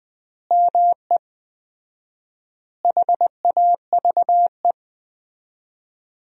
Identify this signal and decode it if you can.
Morse code, heard as a keyed tone, about 20 wpm, 700 Hz. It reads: ME HAVE
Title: 4x Word-Spacing - Sentences from Top 300 Words - 20wpm